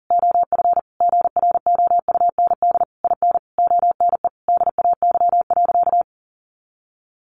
Morse code, heard as a keyed tone, about 37 wpm, 700 Hz. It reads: OF GROUND IN ONE DAY.
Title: Sentences from Top 500 Words 40wpm